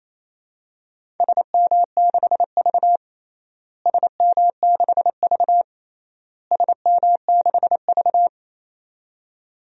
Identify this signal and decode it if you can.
Morse code, heard as a keyed tone, about 28 wpm, 700 Hz. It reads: SM6V SM6V SM6V
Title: DXCC-100 Call Signs - 4 Characters - Repeated 3x - 28wpm